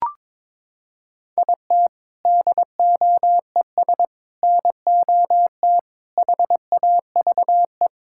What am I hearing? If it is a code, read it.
Morse code, heard as a keyed tone, about 22 wpm, 700 Hz. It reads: IT DOES NOT HAVE
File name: Sentences with 7-Word Phrases - 22wpm